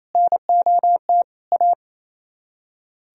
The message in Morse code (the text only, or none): NOT A